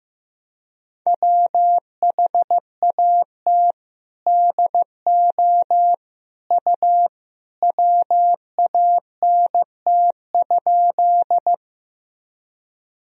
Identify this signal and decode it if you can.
Morse code, heard as a keyed tone, about 16 wpm, 700 Hz. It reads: WHAT DO U WANT?
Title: Sentences from Top 200 Words and Abbreviations 15wpm